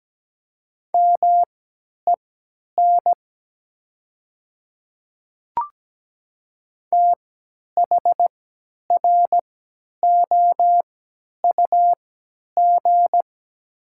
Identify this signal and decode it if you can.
Morse code, heard as a keyed tone, about 17 wpm, spaced as for 10 wpm, 700 Hz. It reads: MEN THROUG
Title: 3x Character Spacing - Top 300 Words - 17wpm